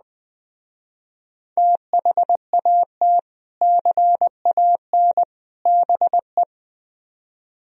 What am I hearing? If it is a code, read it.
Morse code, heard as a keyed tone, about 20 wpm, 700 Hz. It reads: THAT CAN BE
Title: Speed-Racing - Sets of 3 Words 17wpm